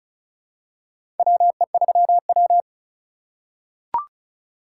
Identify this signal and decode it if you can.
Morse code, heard as a keyed tone, about 34 wpm, 700 Hz. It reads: WE3W
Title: USA Call Signs - 4 Characters - 2x Character Spacing - 35wpm